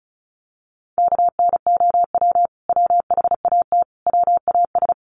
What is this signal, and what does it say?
Morse code, heard as a keyed tone, about 34 wpm, 700 Hz. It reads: KNOW WHAT WAS
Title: Sets of 5 Words Encore2 35wpm